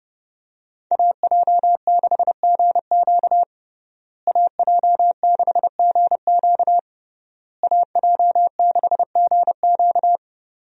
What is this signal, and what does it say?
Morse code, heard as a keyed tone, about 32 wpm, 700 Hz. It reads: AJ6GQ AJ6GQ AJ6GQ
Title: USA Call Signs - 5 Characters - Repeated 3x - 30wpm